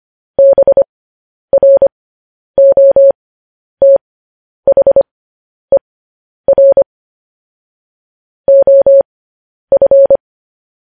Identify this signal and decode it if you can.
Morse code, heard as a keyed tone, about 25 wpm, spaced as for 10 wpm, 550 Hz. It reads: BROTHER OF
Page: news headlines in morse code at 10 wpm